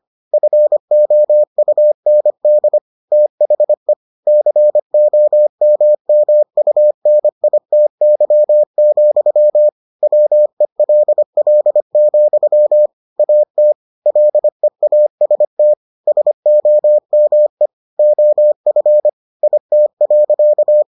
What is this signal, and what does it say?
Morse code, heard as a keyed tone, about 25 wpm, 600 Hz. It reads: FOUND THE COMMUNITY, WELL, AT LEAST SOME OF IT.